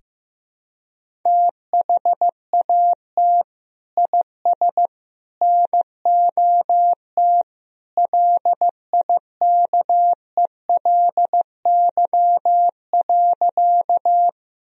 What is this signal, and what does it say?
Morse code, heard as a keyed tone, about 15 wpm, 700 Hz. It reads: THAT IS NOT LIKELY.
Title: Sentences from Top 1000 Words - Review Part 4 - 15wpm